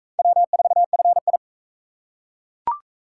Morse code, heard as a keyed tone, about 41 wpm, 700 Hz. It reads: W4FI